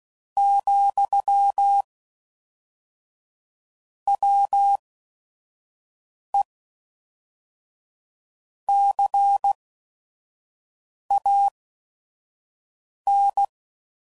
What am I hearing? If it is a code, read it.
Morse code, heard as a keyed tone, about 16 wpm, spaced as for 5 wpm, 800 Hz. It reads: , WE CAN
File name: CW Code Groups @5WPM